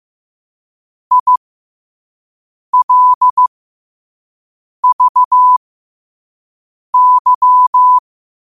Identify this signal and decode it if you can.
Morse code, heard as a keyed tone, about 15 wpm, spaced as for 5 wpm, 1000 Hz. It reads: ILVY
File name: qotdfiles / 5wpm